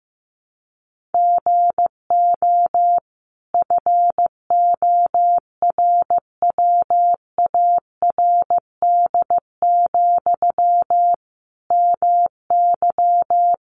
Morse code, heard as a keyed tone, about 15 wpm, 700 Hz. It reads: GO FORWARD, MY
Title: Sentences from Top 1300 Words 15wpm